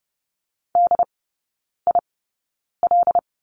Morse code, heard as a keyed tone, about 30 wpm, spaced as for 9 wpm, 700 Hz. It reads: DIL
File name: course / 30:10wpm / 087 Lesson 15e - Number 5 - Three Characters 30wpm